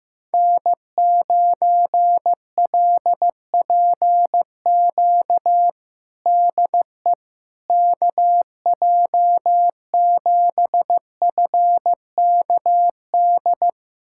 Morse code, heard as a keyed tone, about 15 wpm, 700 Hz. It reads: N9LPQ DE KJ7FKD
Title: USA Callsign DE Callsign 15wpm